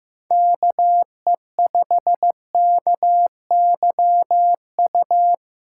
Morse code, heard as a keyed tone, about 15 wpm, 700 Hz. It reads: KE5KYU
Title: USA Callsign DE Callsign 15wpm